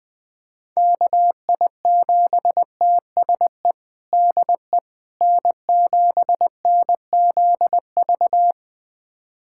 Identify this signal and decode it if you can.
Morse code, heard as a keyed tone, about 20 wpm, 700 Hz. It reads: KI7TSE DE N7NZV
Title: USA Callsign DE Callsign 20wpm